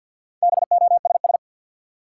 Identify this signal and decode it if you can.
Morse code, heard as a keyed tone, about 52 wpm, 700 Hz. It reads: BOSS
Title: Sentences from Top 700 Words - New Word Drill - 50wpm